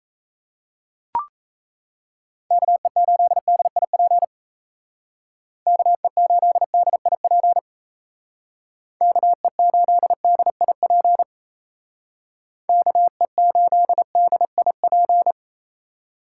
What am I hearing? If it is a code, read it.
Morse code, heard as a keyed tone, about 34 wpm, 700 Hz. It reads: KE8DIP KE8DIP KE8DIP KE8DIP
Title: Speed Racing - USA 2x3 Call Signs 28wpm